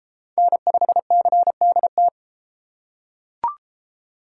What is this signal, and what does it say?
Morse code, heard as a keyed tone, about 32 wpm, 700 Hz. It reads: N5CDT